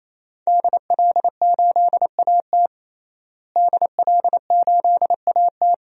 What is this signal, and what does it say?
Morse code, heard as a keyed tone, about 28 wpm, 700 Hz. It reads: DL8AT DL8AT
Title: DXCC-100 Call Signs - 5 Characters - Repeated 3x - 28wpm